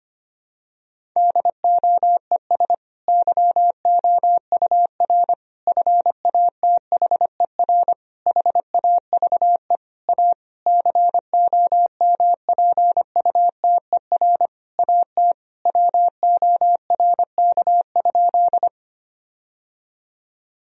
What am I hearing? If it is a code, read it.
Morse code, heard as a keyed tone, about 25 wpm, 700 Hz. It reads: DOES YOUR FATHER HAVE A COMPUTER AT WORK?
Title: Sentences from Top 1000 Words - Review Part 3 - 25wpm